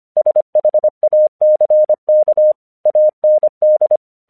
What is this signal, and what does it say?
Morse code, heard as a keyed tone, about 25 wpm, 610 Hz. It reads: SHACK AND
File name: 1 English 25wpm / AQC28